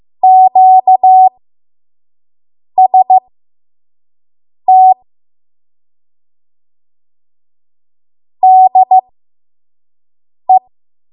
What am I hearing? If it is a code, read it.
Morse code, heard as a keyed tone, about 15 wpm, spaced as for 5 wpm, 750 Hz. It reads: QST DE
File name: Morse / 05WPM